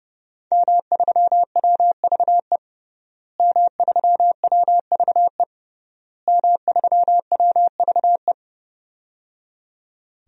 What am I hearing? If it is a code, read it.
Morse code, heard as a keyed tone, about 30 wpm, 700 Hz. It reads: M3WVE M3WVE M3WVE
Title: DXCC-100 Call Signs - 5 Characters - Repeated 3x - 30wpm